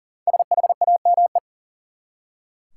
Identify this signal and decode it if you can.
Morse code, heard as a keyed tone, about 40 wpm, 700 Hz. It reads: SHAME
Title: Five Letter Words - 40wpm